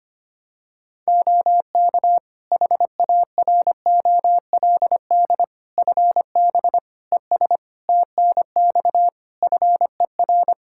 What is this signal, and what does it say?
Morse code, heard as a keyed tone, about 25 wpm, 700 Hz. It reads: OK HAROLD FB ES TNX FER
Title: Practice QSO - Exchange 2 - Info - 25wpm